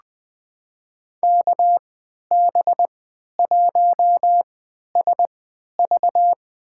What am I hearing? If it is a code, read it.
Morse code, heard as a keyed tone, about 20 wpm, spaced as for 11 wpm, 700 Hz. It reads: KB1SV
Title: USA Call Signs - 3x Character Spacing - 20wpm